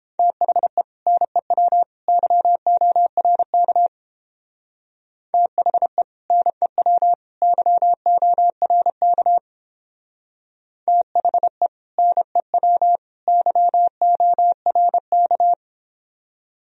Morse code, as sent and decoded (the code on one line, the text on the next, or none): - .... / -. .-- / -.-- --- .-. -.- / - .... / -. .-- / -.-- --- .-. -.- / - .... / -. .-- / -.-- --- .-. -.-
THE NEW YORK THE NEW YORK THE NEW YORK